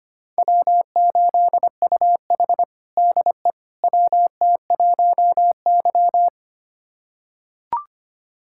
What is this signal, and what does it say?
Morse code, heard as a keyed tone, about 25 wpm, 700 Hz. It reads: W8UH DE WT1Y
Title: USA Callsign DE Callsign - 4 Characters - 25wpm